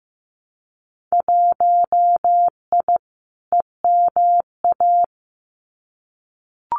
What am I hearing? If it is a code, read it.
Morse code, heard as a keyed tone, about 15 wpm, 700 Hz. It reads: 1I EMA
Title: Winter Field Day Exchanges Encore 15wpm